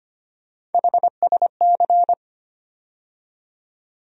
Morse code, heard as a keyed tone, about 25 wpm, 700 Hz. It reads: HSC
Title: USA Call Sign Postfixes - 3 Letters 25wpm